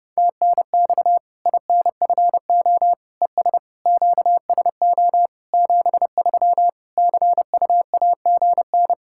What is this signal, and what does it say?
Morse code, heard as a keyed tone, about 32 wpm, 700 Hz. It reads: TNX INFO ES QSO 73 CUAGN
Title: K1USN SST QSO Practice 30wpm